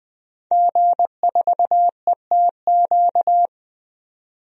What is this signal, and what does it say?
Morse code, heard as a keyed tone, about 20 wpm, 700 Hz. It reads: G4ETQ
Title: DXCC-100 Call Signs - 5 Characters - 20wpm